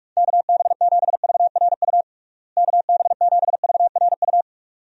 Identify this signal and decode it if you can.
Morse code, heard as a keyed tone, about 45 wpm, 700 Hz. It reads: KB7VRU KB7VRU
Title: USA Call Signs - 6 Characters - Repeated 2x - 45wpm